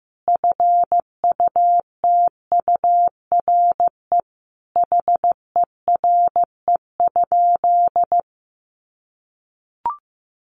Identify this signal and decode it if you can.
Morse code, heard as a keyed tone, about 15 wpm, 700 Hz. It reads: FUTURE HERE?
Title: Sentences from Top 1500 Words 15wpm